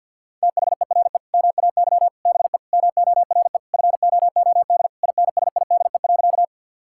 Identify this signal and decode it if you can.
Morse code, heard as a keyed tone, about 52 wpm, 700 Hz. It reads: THERE MAY BE MORE FOOD INSIDE.
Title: Sentences from Top 1000 Words - Review Part 6 - 50wpm